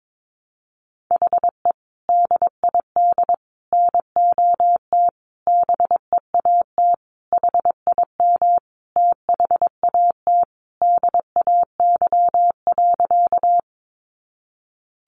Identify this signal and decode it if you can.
Morse code, heard as a keyed tone, about 22 wpm, 700 Hz. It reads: HE DID NOT BEAT HIM THAT DAY.
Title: Sentences from Top 1300 Words 22wpm